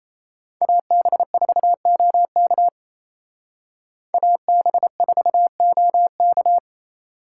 Morse code, heard as a keyed tone, about 35 wpm, 700 Hz. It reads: AB4OK AB4OK